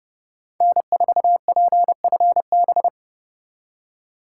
Morse code, heard as a keyed tone, about 40 wpm, 700 Hz. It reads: N4PFB